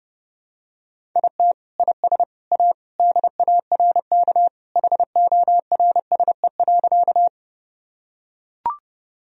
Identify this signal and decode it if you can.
Morse code, heard as a keyed tone, about 30 wpm, 700 Hz. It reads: IT IS A DARK HORSE.
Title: Sentences from Top 1000 Words - Review Part 1 - 30wpm